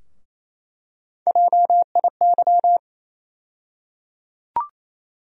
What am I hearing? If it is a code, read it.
Morse code, heard as a keyed tone, about 27 wpm, 700 Hz. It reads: JIY
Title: USA Call Sign Postfixes - 3 Letters 28wpm